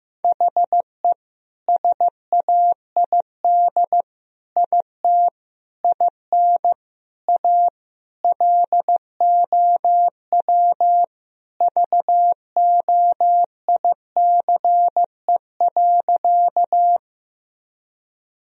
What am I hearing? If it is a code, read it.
Morse code, heard as a keyed tone, about 16 wpm, 700 Hz. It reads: HE SAID IT IN A LOW VOICE.
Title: Sentences from Top 1000 Words - Review Part 1 - 15wpm